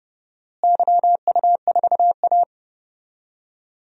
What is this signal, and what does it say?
Morse code, heard as a keyed tone, about 29 wpm, 700 Hz. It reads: YU4A
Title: DXCC-100 Call Signs - 4 Characters - Repeated 3x - 30wpm